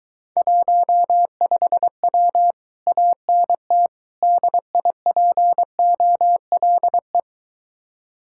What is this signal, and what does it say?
Morse code, heard as a keyed tone, about 23 wpm, 700 Hz. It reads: 15W ANT DIPOLE